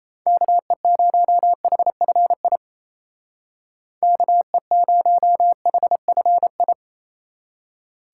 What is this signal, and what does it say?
Morse code, heard as a keyed tone, about 37 wpm, 700 Hz. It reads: KE0HFI KE0HFI